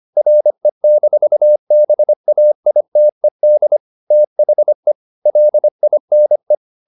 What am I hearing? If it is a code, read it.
Morse code, heard as a keyed tone, about 25 wpm, 600 Hz. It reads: RE-BAITED THE LINE